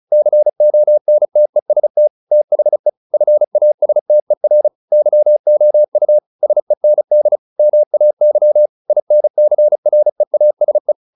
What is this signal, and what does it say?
Morse code, heard as a keyed tone, about 34 wpm, 600 Hz. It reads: CONTEST THE FASTER YOU SEND MAY INCREASE